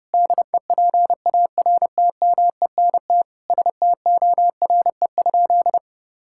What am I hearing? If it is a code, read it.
Morse code, heard as a keyed tone, about 30 wpm, 700 Hz. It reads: DEPARTMENT STORE?
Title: Sentences from Top 1000 Words - Review Part 5 - 30wpm